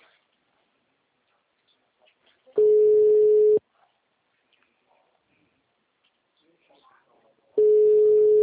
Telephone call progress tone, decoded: ringback tone

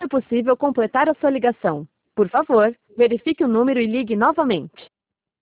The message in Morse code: none

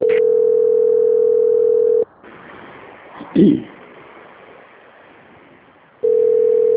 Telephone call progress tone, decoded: ringback tone

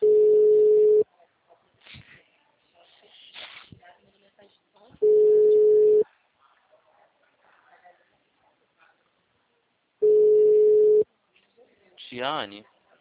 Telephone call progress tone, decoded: ringback tone

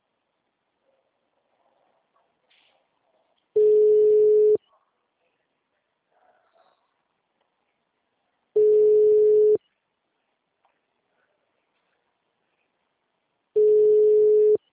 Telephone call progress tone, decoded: ringback tone

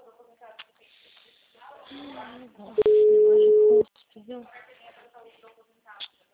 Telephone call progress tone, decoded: ringback tone

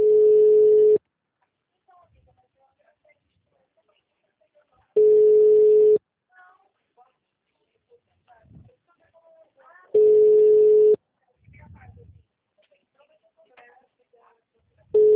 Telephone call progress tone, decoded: ringback tone

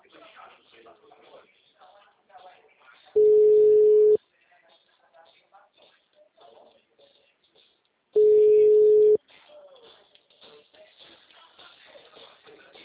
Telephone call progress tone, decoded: ringback tone

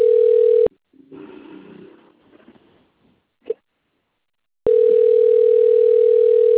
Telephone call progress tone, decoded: ringback tone

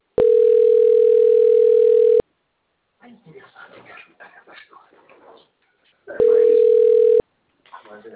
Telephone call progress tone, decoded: ringback tone

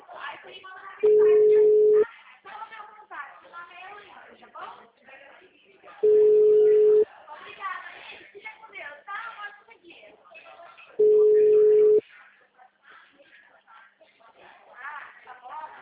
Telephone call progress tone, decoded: ringback tone

